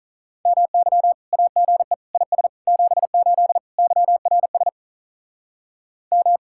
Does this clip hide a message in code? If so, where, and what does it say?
Yes: Morse code from 0.5 to 6.4 s, heard as a keyed tone, about 48 wpm, 700 Hz. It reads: MY AGE IS 78 YRS M